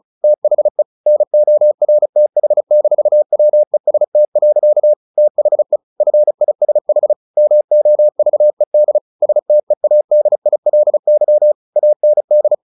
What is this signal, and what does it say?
Morse code, heard as a keyed tone, about 34 wpm, 600 Hz. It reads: THE NORTH-WEST. THE FISH MOVED STEADILY AND
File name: the old man and the sea / the old man and the sea 1 35wpm